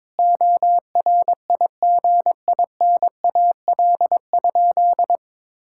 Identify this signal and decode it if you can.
Morse code, heard as a keyed tone, about 22 wpm, 700 Hz. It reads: ORIGINAL?
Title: Sentences from Top 1000 Words - Review Part 6 - 22wpm